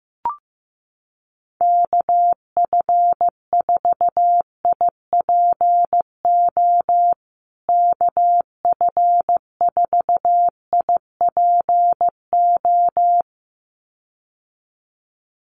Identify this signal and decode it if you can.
Morse code, heard as a keyed tone, about 15 wpm, 700 Hz. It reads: KF4IPO KF4IPO
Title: USA Call Signs Repeated 2x 15wpm